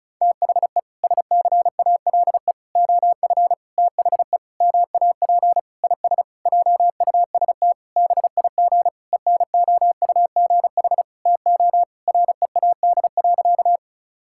Morse code, heard as a keyed tone, about 34 wpm, 700 Hz. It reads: THE SCALE OF THE MAP IS JUST BIG ENOUGH TO READ.
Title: Sentences from Top 1000 Words - Review Part 5 - 35wpm